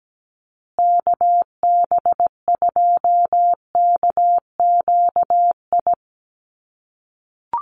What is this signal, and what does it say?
Morse code, heard as a keyed tone, about 17 wpm, 700 Hz. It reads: KB2KQI
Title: USA Call Signs Repeated 3x 17wpm